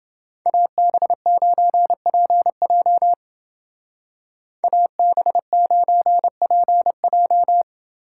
Morse code, heard as a keyed tone, about 28 wpm, 700 Hz. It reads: AB9PJ AB9PJ